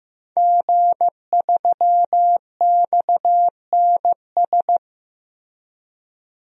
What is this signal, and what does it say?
Morse code, heard as a keyed tone, about 15 wpm, 700 Hz. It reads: G3XNS
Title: DXCC-100 Call Signs - 5 Characters - Repeated 3x - 15wpm